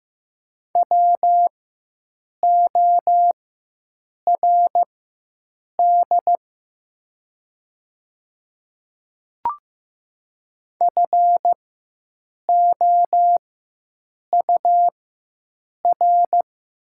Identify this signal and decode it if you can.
Morse code, heard as a keyed tone, about 15 wpm, spaced as for 7 wpm, 700 Hz. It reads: WORD FOUR
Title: 4x Character Spacing - Top 300 Words - 15wpm